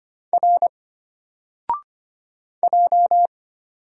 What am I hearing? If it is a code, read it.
Morse code, heard as a keyed tone, about 25 wpm, 700 Hz. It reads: R J
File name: Single-Letter-Number-25wpm